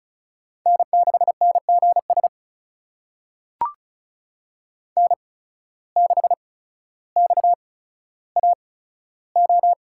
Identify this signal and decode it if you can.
Morse code, heard as a keyed tone, about 36 wpm, spaced as for 15 wpm, 700 Hz. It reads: N6NGS N6XAO
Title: USA Call Signs - 5 Characters - 8x Character Spacing - 35wpm